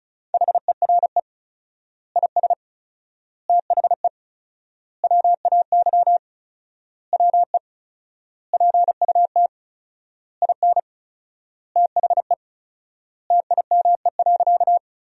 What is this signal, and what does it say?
Morse code, heard as a keyed tone, about 34 wpm, 700 Hz. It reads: HERE IS THE WAY WE PUT IN THE TIME.
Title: 4x Word-Spacing - Sentences from Top 200 Words - 35wpm